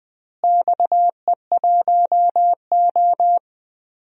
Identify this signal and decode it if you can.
Morse code, heard as a keyed tone, about 20 wpm, 700 Hz. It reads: XE1O
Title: DXCC-100 Call Signs - 4 Characters - Repeated 2x - 20wpm